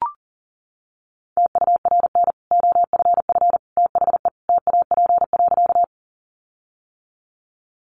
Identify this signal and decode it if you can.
Morse code, heard as a keyed tone, about 37 wpm, 700 Hz. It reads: TURN OFF THE TAP.
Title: Sentences from Top 1400 Words 40wpm